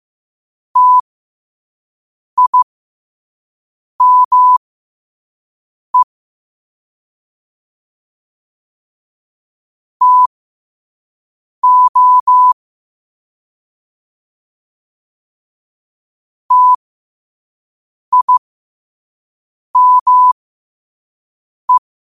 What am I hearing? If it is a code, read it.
Morse code, heard as a keyed tone, about 15 wpm, spaced as for 5 wpm, 1000 Hz. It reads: TIME TO TIME
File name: qotdfiles / 5wpm